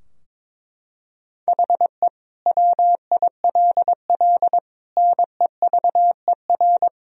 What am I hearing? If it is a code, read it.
Morse code, heard as a keyed tone, about 22 wpm, 700 Hz. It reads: HE WILL NEVER
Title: Sentences from Top 1000 Words - Review Part 2 - 22wpm